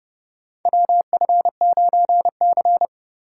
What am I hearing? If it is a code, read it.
Morse code, heard as a keyed tone, about 30 wpm, 700 Hz. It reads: WF9C
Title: USA Call Signs - 4 Characters - 2x Character Spacing - 30wpm